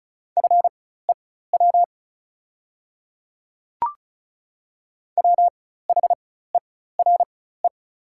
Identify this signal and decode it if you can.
Morse code, heard as a keyed tone, about 36 wpm, spaced as for 16 wpm, 690 Hz. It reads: FEW WHERE
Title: 4x Character Spacing - Top 300 Words - 35wpm